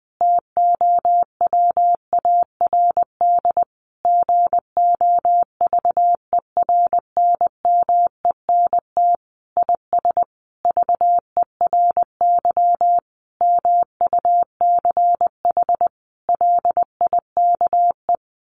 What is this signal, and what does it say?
Morse code, heard as a keyed tone, about 20 wpm, 700 Hz. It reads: TOWARD GOVERNMENT IS VERY MUCH LIKE
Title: Sentences from Top 1300 Words 20wpm